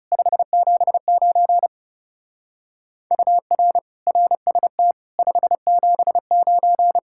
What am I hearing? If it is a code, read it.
Morse code, heard as a keyed tone, about 36 wpm, 700 Hz. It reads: 579 UR RST 579
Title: Speed-Racing Common QSO Phrases Encore - 30wpm